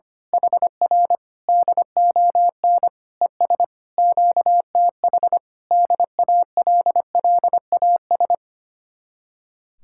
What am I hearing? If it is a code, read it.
Morse code, heard as a keyed tone, about 25 wpm, 700 Hz. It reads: HR DON ES QTH DALLAS
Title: Speed-Racing Common QSO Phrases Encore - 25wpm